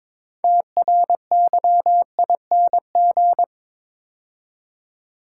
Morse code, heard as a keyed tone, about 22 wpm, 700 Hz. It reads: TRYING
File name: practice / 22wpm / Build A Phrase - 5-Word Phrases - 22wpm